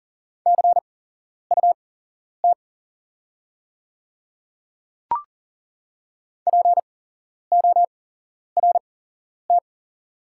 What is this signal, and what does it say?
Morse code, heard as a keyed tone, about 40 wpm, spaced as for 11 wpm, 700 Hz. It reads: CUT PORT
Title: 8x Character Spacing - Top 500 Words - 40wpm